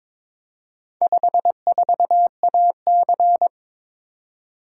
Morse code, heard as a keyed tone, about 22 wpm, 700 Hz. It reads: 54AC
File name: practice / 22wpm / Four Letter-Number Combinations - 22wpm